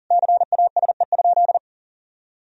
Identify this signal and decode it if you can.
Morse code, heard as a keyed tone, about 37 wpm, 700 Hz. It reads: CASE?
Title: Sentences from Top 1000 Words - Review Part 2 - 40wpm